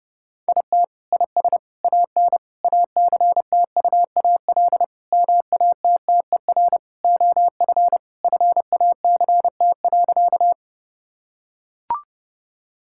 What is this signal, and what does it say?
Morse code, heard as a keyed tone, about 30 wpm, 700 Hz. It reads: IT IS AN ACTUAL MATTER OF FACT.
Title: Sentences from Top 1000 Words - Review Part 5 - 30wpm